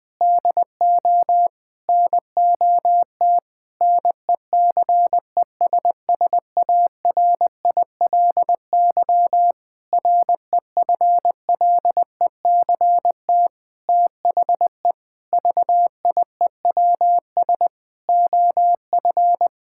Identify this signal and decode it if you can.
Morse code, heard as a keyed tone, about 21 wpm, 700 Hz. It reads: DO NOT NECESSARILY REFLECT THE VIEWS OF
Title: Sentences with 8-Word Phrases - 20wpm